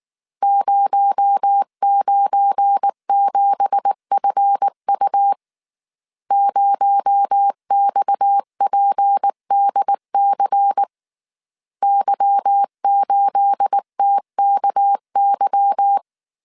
Morse code, heard as a keyed tone, about 19 wpm, 800 Hz. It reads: 097FU 0XPDC Y8TKY
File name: CW / DET / 1 9 1500C S A 18wpm 2459